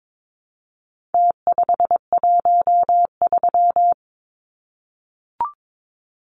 Morse code, heard as a keyed tone, about 22 wpm, 700 Hz. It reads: T513
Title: Numbers - Contest Style - 4-Digit 22wpm